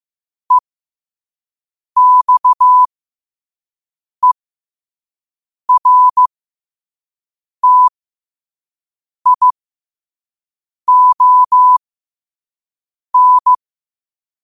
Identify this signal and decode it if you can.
Morse code, heard as a keyed tone, about 15 wpm, spaced as for 5 wpm, 1000 Hz. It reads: EXERTION
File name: qotdfiles / 5wpm